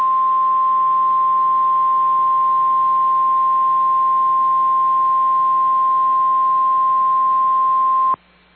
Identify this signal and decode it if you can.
Emergency Alert System (SAME) tones (the 1050 Hz attention tone). It is heard from 0.7 to 8.2 s.
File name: EAS Logs / KMNR